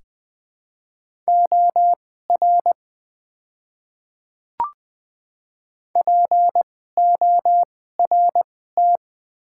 Morse code, heard as a keyed tone, about 20 wpm, spaced as for 14 wpm, 700 Hz. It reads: OR PORT
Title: 2x Character Spacing - Top 300 Words - 20wpm